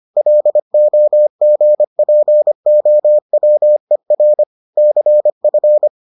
Morse code, heard as a keyed tone, about 25 wpm, 600 Hz. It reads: LOGPOWER CF